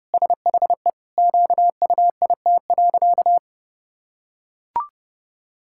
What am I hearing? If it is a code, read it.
Morse code, heard as a keyed tone, about 30 wpm, 700 Hz. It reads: SHE QUIT.
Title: Pangram Challenge v2 - 30wpm